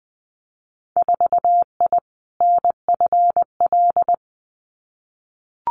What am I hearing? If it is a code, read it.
Morse code, heard as a keyed tone, about 20 wpm, 700 Hz. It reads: 4I NFL
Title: Winter Field Day Exchanges 20wpm